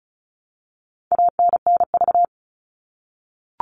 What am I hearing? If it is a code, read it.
Morse code, heard as a keyed tone, about 34 wpm, 700 Hz. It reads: ANNV
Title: All Cut Numbers - 4-Digit 35wpm